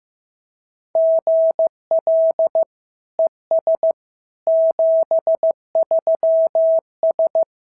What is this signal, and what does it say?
Morse code, heard as a keyed tone, about 14 wpm, 650 Hz. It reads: GL ES 73S
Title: SOTA QSO Practice 15wpm